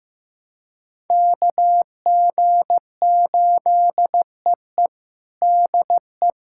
Morse code, heard as a keyed tone, about 15 wpm, 700 Hz. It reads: KG8EE DE
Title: USA Callsign DE Callsign 15wpm